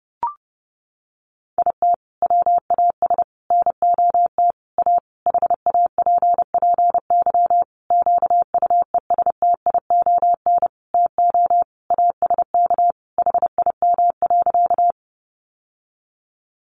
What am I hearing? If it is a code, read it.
Morse code, heard as a keyed tone, about 30 wpm, 700 Hz. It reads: IT WAS NOT A HAPPY QUESTION TO ASK HIM.
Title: Sentences from Top 1000 Words 30wpm